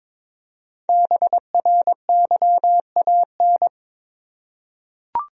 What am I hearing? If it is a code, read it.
Morse code, heard as a keyed tone, about 21 wpm, 700 Hz. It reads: BRYAN
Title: USA Common Names - Max 5 Letters - 22wpm